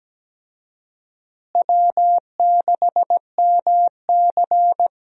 Morse code, heard as a keyed tone, about 17 wpm, 700 Hz. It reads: W6MC
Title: USA Call Signs - 4 Characters - Repeated 2x - 17wpm